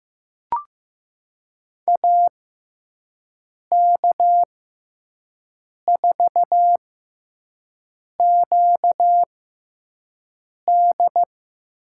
Morse code, heard as a keyed tone, about 15 wpm, spaced as for 5 wpm, 700 Hz. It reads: AK4QD
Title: USA Call Signs - 6x Character Spacing - 15wpm